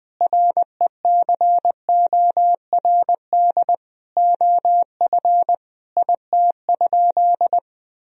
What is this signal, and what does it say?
Morse code, heard as a keyed tone, about 20 wpm, 700 Hz. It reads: RECORD OF IT?
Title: Sentences from Top 1000 Words - Review Part 3 - 20wpm